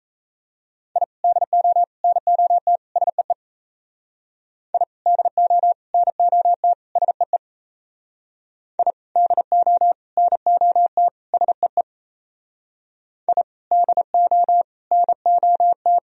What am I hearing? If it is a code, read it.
Morse code, heard as a keyed tone, about 36 wpm, 700 Hz. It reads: I DO NOT SEE I DO NOT SEE I DO NOT SEE I DO NOT